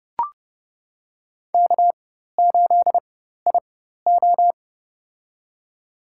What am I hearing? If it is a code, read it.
Morse code, heard as a keyed tone, about 32 wpm, spaced as for 14 wpm, 700 Hz. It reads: K8IO
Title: USA Call Signs - 4 Characters - 4x Character Spacing - 30wpm